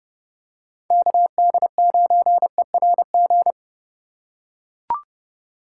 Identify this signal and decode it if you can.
Morse code, heard as a keyed tone, about 32 wpm, 700 Hz. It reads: KD9ERG
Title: USA Call Signs - 6 Characters - 2x Character Spacing - 30wpm